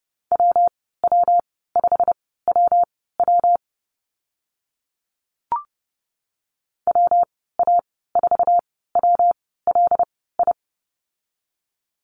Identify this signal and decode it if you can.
Morse code, heard as a keyed tone, about 30 wpm, spaced as for 17 wpm, 700 Hz. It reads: WW5WW WA4WLI
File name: Lesson 19g - Letter w - Call Signs 30wpm